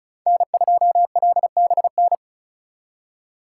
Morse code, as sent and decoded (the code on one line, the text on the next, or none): -. ..--- .-.. -... -.
N2LBN